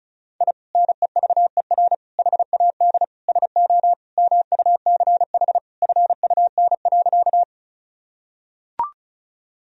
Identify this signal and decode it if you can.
Morse code, heard as a keyed tone, about 36 wpm, 700 Hz. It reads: I NEVER HAD SO MUCH FUN.
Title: Sentences from Top 1000 Words - Review Part 4 - 35wpm